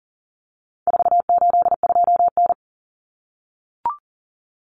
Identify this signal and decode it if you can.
Morse code, heard as a keyed tone, about 40 wpm, 700 Hz. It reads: 482N
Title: Common Cut Numbers - 4-Digit 40wpm